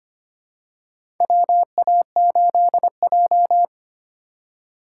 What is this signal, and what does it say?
Morse code, heard as a keyed tone, about 25 wpm, 700 Hz. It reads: WA8J
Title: USA Call Signs - 4 Characters - 4x Character Spacing - 25wpm